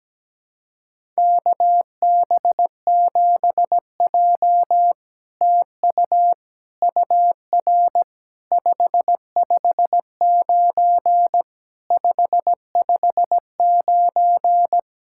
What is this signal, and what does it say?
Morse code, heard as a keyed tone, about 17 wpm, 700 Hz. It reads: KB7J TU UR 559 559
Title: SOTA QSO Practice 17wpm